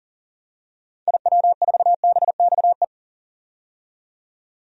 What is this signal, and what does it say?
Morse code, heard as a keyed tone, about 40 wpm, 700 Hz. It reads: IW4BXE
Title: DXCC-100 Call Signs - 6 Characters - 40wpm